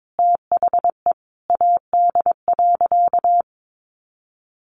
Morse code, heard as a keyed tone, about 22 wpm, 700 Hz. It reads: THE AD.
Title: Sentences from Top 1300 Words 22wpm